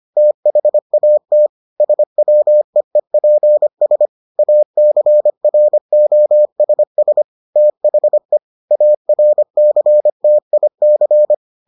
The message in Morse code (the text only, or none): THAT SWEEPS ACROSS THE ARCTIC